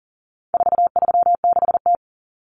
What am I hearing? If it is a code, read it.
Morse code, heard as a keyed tone, about 40 wpm, 700 Hz. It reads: 436T
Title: Common Cut Numbers - 4-Digit 40wpm